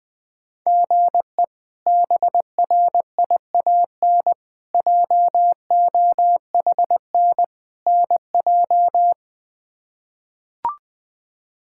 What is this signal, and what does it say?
Morse code, heard as a keyed tone, about 20 wpm, 700 Hz. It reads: GE BRIAN JOHN NJ
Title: K1USN SST QSO Practice - 20wpm